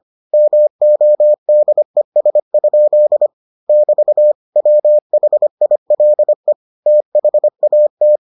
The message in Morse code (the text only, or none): MODES? <BT> WHILE THAT